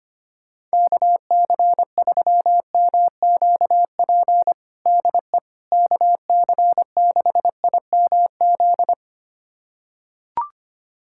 Morse code, heard as a keyed tone, about 25 wpm, 700 Hz. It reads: KC3MQP DE KC6IMZ
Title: USA Callsign DE Callsign 25wpm